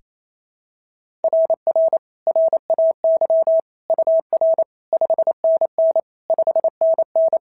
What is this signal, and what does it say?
Morse code, heard as a keyed tone, about 27 wpm, 650 Hz. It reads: RR RAY UR 5NN 5NN
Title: SOTA QSO Practice 28wpm